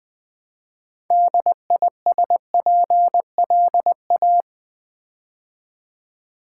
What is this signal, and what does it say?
Morse code, heard as a keyed tone, about 20 wpm, 700 Hz. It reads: DISPLA
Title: Build A Word - 3 Letters At A Time - 8-Letter Words - 20wpm